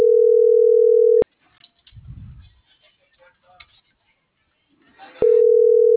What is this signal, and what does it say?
Ringback tone (North America), 440 + 480 Hz.